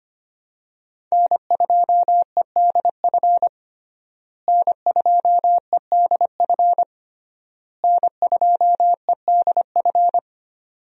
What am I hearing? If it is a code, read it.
Morse code, heard as a keyed tone, about 25 wpm, 700 Hz. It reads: N2EDF N2EDF N2EDF
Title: USA Call Signs - 5 Characters - Repeated 3x - 25wpm